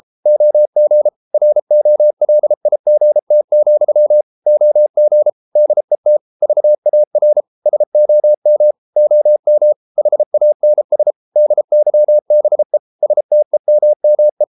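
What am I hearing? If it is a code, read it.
Morse code, heard as a keyed tone, about 33 wpm, 600 Hz. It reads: OG ROLIGT, OG DET VAR SOM OM HANS DYBE STEMME
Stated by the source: Morse code at 33 wpm, QRQ.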